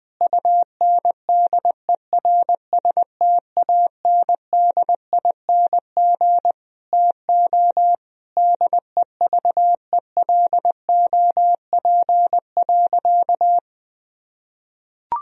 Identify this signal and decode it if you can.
Morse code, heard as a keyed tone, about 20 wpm, 700 Hz. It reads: UNDERSTANDING TO DEVELOP.
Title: Sentences from Top 1000 Words - Review Part 3 - 20wpm